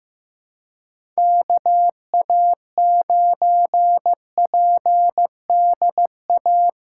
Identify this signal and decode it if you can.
Morse code, heard as a keyed tone, about 15 wpm, 700 Hz. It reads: KA9PDA
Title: USA Call Signs - 6 Characters - 8x Character Spacing - 15wpm